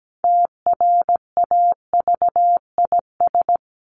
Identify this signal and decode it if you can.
Morse code, heard as a keyed tone, about 17 wpm, 700 Hz. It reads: TRAVIS
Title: A Sound Of Thunder 17wpm